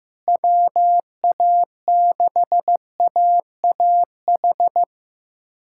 Morse code, heard as a keyed tone, about 15 wpm, 700 Hz. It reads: WA6AAH
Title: USA Callsign DE Callsign 15wpm